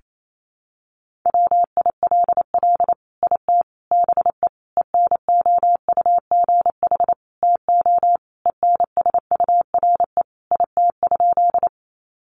Morse code, heard as a keyed tone, about 28 wpm, 700 Hz. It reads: WILL IT BE ENOUGH TO ENSURE IT?
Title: Sentences from Top 1100 Words 28wpm